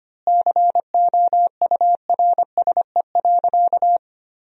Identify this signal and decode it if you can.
Morse code, heard as a keyed tone, about 25 wpm, 700 Hz. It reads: COURSE.
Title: 2x Word-Spacing - Sentences from Top 500 Words - 25wpm